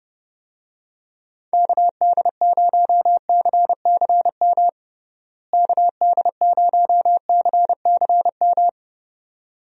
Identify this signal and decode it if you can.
Morse code, heard as a keyed tone, about 31 wpm, 700 Hz. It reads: KD0CCM KD0CCM
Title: USA Call Signs - 6 Characters - Repeated 2x - 30wpm